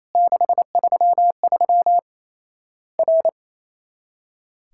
Morse code, heard as a keyed tone, about 28 wpm, 700 Hz. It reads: 633 R
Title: Worked All Europe DX Contest - QTC Exchange - 28wpm